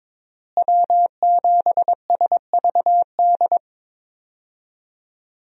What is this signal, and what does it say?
Morse code, heard as a keyed tone, about 21 wpm, 700 Hz. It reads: W7SVD
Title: USA Call Signs - 5 Characters - Repeated 2x - 22wpm